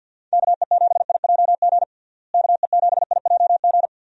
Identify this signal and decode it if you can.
Morse code, heard as a keyed tone, about 52 wpm, 700 Hz. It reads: KE7IJG KE7IJG
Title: USA Call Signs - 6 Characters - Repeated 2x - 50wpm